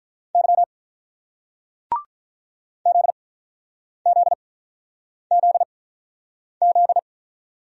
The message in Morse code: Y ZZZZ

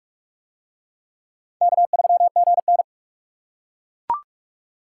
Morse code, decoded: K3GN

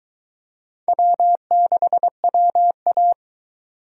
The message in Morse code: W6WA

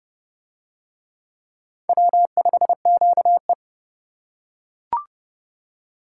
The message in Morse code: W5QE

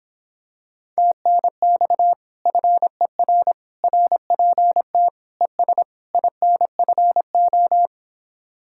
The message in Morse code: TNX FER RPT ES INFO

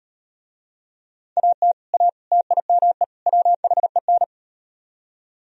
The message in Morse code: AT A TIME WHEN

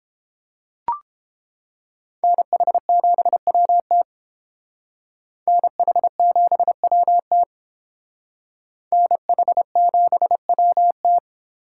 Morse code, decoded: NH7WT NH7WT NH7WT

